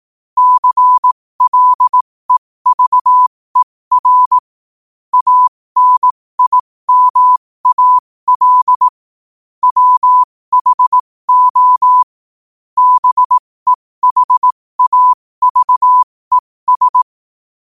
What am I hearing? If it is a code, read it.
Morse code, heard as a keyed tone, about 18 wpm, spaced as for 15 wpm, 1000 Hz. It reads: CLEVER ANIMAL WHO BEHAVES
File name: qotdfiles / 15wpm